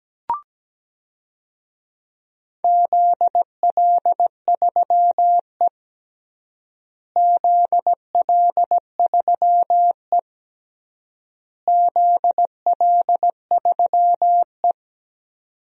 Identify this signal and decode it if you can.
Morse code, heard as a keyed tone, about 17 wpm, 700 Hz. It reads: ZL3E ZL3E ZL3E
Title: DXCC-100 Call Signs - 4 Characters - Repeated 3x - 17wpm